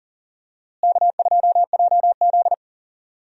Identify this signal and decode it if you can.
Morse code, heard as a keyed tone, about 40 wpm, 700 Hz. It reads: K2JZ
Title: USA Call Signs - 4 Characters - Repeated 3x - 40wpm